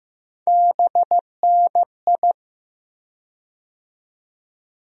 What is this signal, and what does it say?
Morse code, heard as a keyed tone, about 15 wpm, 700 Hz. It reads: BNI
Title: USA Call Sign Postfixes - 3 Letters 15wpm